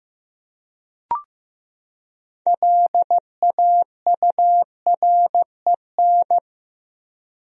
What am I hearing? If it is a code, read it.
Morse code, heard as a keyed tone, about 15 wpm, 700 Hz. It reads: LAUREN
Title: USA Common Names - 15wpm